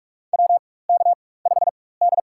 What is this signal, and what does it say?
Morse code, heard as a keyed tone, about 44 wpm, spaced as for 21 wpm, 700 Hz. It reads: WK5D
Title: USA Call Signs - 4 Characters - 4x Character Spacing - 45wpm